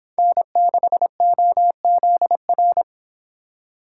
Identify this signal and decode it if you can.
Morse code, heard as a keyed tone, about 28 wpm, 700 Hz. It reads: N6OZR